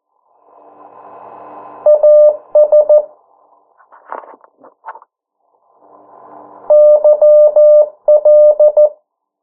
Morse code, heard as a keyed tone, about 14 wpm, 600 Hz. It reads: AS YL
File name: morse mp3 / 09 urok 8